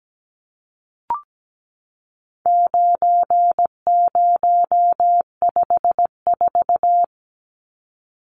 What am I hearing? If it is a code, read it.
Morse code, heard as a keyed tone, about 17 wpm, 700 Hz. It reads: 9054
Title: Numbers - 4-Digit 17wpm